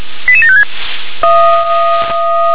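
Five-tone selective call (ZVEI), heard as quick runs of five tones, tones of about 70 ms. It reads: 88756
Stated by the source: ZVEI five-tone selective call received by SDR.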